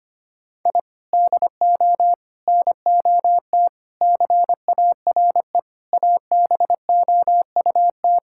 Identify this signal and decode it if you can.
Morse code, heard as a keyed tone, about 25 wpm, 700 Hz. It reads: I DO NOT CARE ABOUT
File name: practice / 25wpm / Sentences with 4-Word Phrases - 25wpm